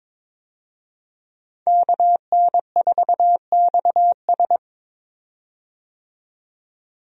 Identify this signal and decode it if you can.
Morse code, heard as a keyed tone, about 22 wpm, 700 Hz. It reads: KN4XS
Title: USA Call Signs - 5 Characters - 22wpm